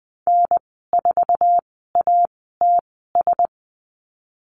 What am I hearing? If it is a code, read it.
Morse code, heard as a keyed tone, about 20 wpm, spaced as for 15 wpm, 700 Hz. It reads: N4ATS
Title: 055 Lesson 10g - Letter r - Call Signs 20wpm